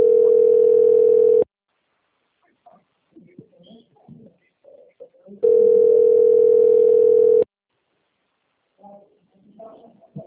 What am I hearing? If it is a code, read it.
Ringback tone (North America), 440 + 480 Hz.